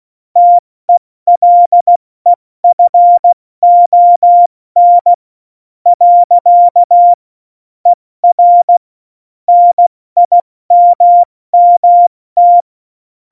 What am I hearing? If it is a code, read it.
Morse code, heard as a keyed tone, about 16 wpm, spaced as for 14 wpm, 700 Hz. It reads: TELEFON . ER NIMMT
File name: Morsekurs / Witze / Witz 02. Oktober 2018, 70 F80